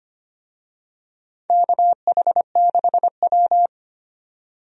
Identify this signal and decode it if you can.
Morse code, heard as a keyed tone, about 25 wpm, 700 Hz. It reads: KH6W
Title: DXCC-100 Call Signs - 4 Characters - 25wpm